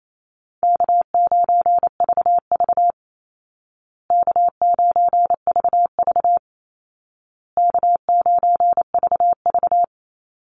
Morse code, heard as a keyed tone, about 28 wpm, 700 Hz. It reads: K9VV K9VV K9VV
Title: Easy Call Signs Repeated 3x - 28wpm